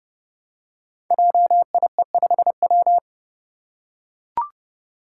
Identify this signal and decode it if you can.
Morse code, heard as a keyed tone, about 29 wpm, 700 Hz. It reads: JIE5W
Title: Code Groups - 5 Letter:Numbers - 30wpm